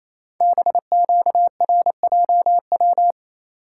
Morse code, heard as a keyed tone, about 28 wpm, 700 Hz. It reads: BQRJW
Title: Code Groups - 5 Letters - 28wpm